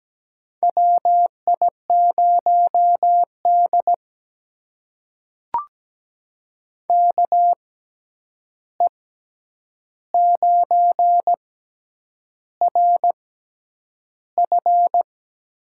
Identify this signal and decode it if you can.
Morse code, heard as a keyed tone, about 17 wpm, spaced as for 8 wpm, 700 Hz. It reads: WI0D KE9RF
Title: USA Call Signs - 6x Character Spacing - 17wpm